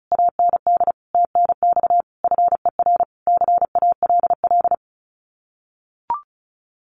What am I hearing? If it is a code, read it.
Morse code, heard as a keyed tone, about 36 wpm, 700 Hz. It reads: AND TNX FER CALL
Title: Common QSO Phrases Encore 35wpm